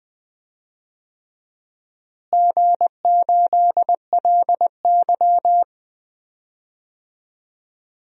Morse code, heard as a keyed tone, about 19 wpm, 700 Hz. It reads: G8LY
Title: DXCC-100 Call Signs - 4 Characters - Repeated 3x - 20wpm